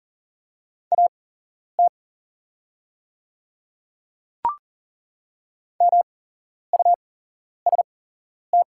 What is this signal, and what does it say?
Morse code, heard as a keyed tone, about 40 wpm, spaced as for 11 wpm, 700 Hz. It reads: AT MUST
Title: 8x Character Spacing - Top 300 Words - 40wpm